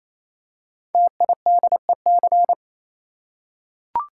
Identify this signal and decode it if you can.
Morse code, heard as a keyed tone, about 27 wpm, 700 Hz. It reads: TIDEC
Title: Code Groups - 5 Letters - 28wpm